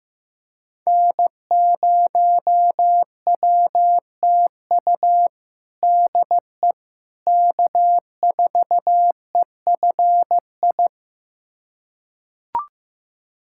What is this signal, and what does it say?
Morse code, heard as a keyed tone, about 15 wpm, 700 Hz. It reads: N0WTU DE K4EFI
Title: USA Callsign DE Callsign - 5 Characters - 15wpm